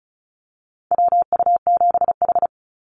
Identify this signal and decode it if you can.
Morse code, heard as a keyed tone, about 34 wpm, 700 Hz. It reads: WU7H
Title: NA SOTA Activator Callsigns 35wpm